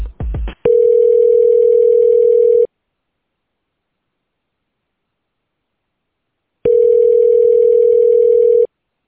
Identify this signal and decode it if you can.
Ringback tone (North America), 440 + 480 Hz.